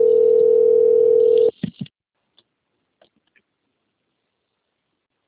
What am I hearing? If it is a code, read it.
Ringback tone (North America), 440 + 480 Hz.